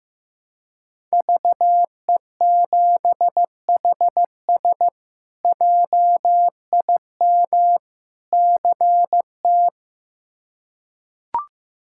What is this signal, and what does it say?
Morse code, heard as a keyed tone, about 15 wpm, 700 Hz. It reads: VE7HS JIM CT
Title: K1USN SST QSO Practice - 15wpm